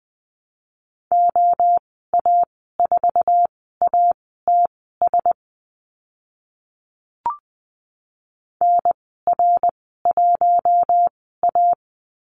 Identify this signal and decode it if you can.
Morse code, heard as a keyed tone, about 20 wpm, spaced as for 14 wpm, 700 Hz. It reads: OA4ATS NR1A
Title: 062 Lesson 11g - Letter h - Call Signs 20wpm